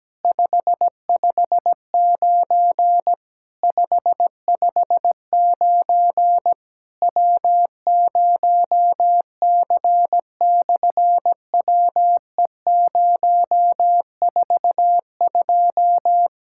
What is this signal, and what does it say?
Morse code, heard as a keyed tone, about 17 wpm, 700 Hz. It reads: 559 559 W0C/WE042
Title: SOTA QSO Practice 17wpm